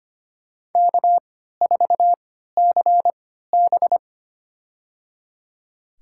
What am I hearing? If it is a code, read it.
Morse code, heard as a keyed tone, about 25 wpm, spaced as for 14 wpm, 700 Hz. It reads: K4CB
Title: USA Call Signs - 3x Character Spacing - 25wpm